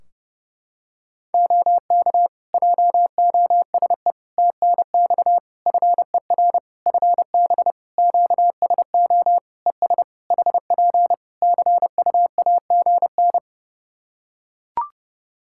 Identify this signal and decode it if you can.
Morse code, heard as a keyed tone, about 30 wpm, 700 Hz. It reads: OK JOSE TNX FER FB QSO ES HP CUAGN
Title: K1USN SST QSO Practice 30wpm